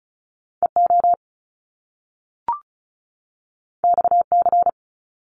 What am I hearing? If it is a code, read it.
Morse code, heard as a keyed tone, about 36 wpm, 700 Hz. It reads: EO XC
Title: Top Two Letter Combinations in Words 35wpm